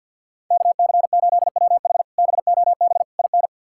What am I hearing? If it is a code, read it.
Morse code, heard as a keyed tone, about 50 wpm, 700 Hz. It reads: KC8WH BOB IN
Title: K1USN SST QSO Practice - 50wpm